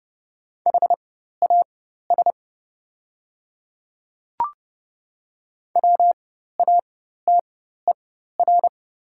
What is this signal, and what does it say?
Morse code, heard as a keyed tone, about 30 wpm, spaced as for 14 wpm, 690 Hz. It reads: HAS WATER
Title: 4x Character Spacing - Top 100 Words - 30wpm